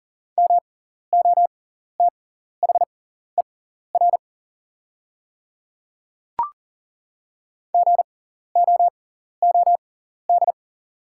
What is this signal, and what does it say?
Morse code, heard as a keyed tone, about 40 wpm, spaced as for 14 wpm, 690 Hz. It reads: MOTHER GOOD
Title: 6x Character Spacing - Top 200 Words - 40wpm